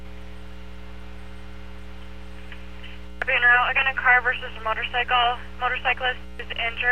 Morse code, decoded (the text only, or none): none